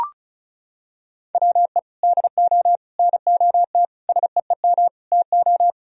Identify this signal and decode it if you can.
Morse code, heard as a keyed tone, about 34 wpm, 700 Hz. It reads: WE DO NOT SEEM TO
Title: Sentences from Top 1000 Words - Review Part 1 - 35wpm